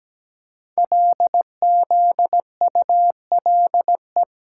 Morse code, heard as a keyed tone, about 17 wpm, 700 Hz. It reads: LZULE
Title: Code Groups - 5 Letters - 17wpm